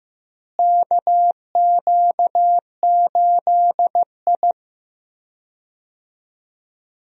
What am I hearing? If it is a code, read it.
Morse code, heard as a keyed tone, about 15 wpm, 700 Hz. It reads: KQ8I
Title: USA Call Signs - 4 Characters - Repeated 3x - 15wpm